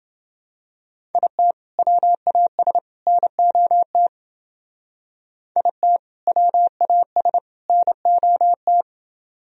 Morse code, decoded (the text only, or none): IT WAS NOT IT WAS NOT